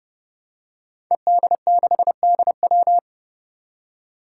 Morse code, heard as a keyed tone, about 30 wpm, 700 Hz. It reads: ED6DW
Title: Five Challenging Letter-Number Combinations 30wpm